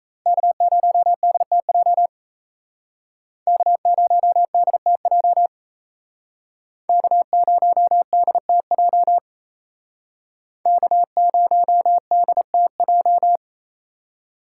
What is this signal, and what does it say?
Morse code, heard as a keyed tone, about 35 wpm, 700 Hz. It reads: K0DTJ K0DTJ K0DTJ K0DTJ